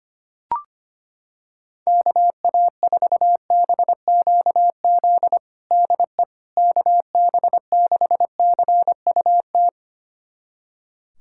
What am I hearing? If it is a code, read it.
Morse code, heard as a keyed tone, about 25 wpm, 700 Hz. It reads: KA4BQZ DE KB6CUT
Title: USA Callsign DE Callsign - 6 Characters - 25wpm